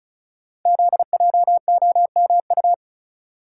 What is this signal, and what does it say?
Morse code, heard as a keyed tone, about 36 wpm, 700 Hz. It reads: ZJOMU
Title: Code Groups - 5 Letters - 35wpm